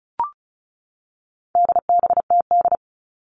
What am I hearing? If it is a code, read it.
Morse code, heard as a keyed tone, about 34 wpm, 700 Hz. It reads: DBTD
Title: All Cut Numbers - 4-Digit 35wpm